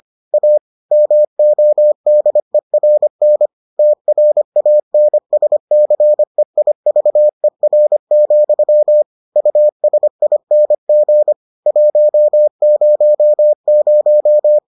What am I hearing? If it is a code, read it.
Morse code, heard as a keyed tone, about 25 wpm, 600 Hz. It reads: A MODERN TRANSCEIVER, USING 100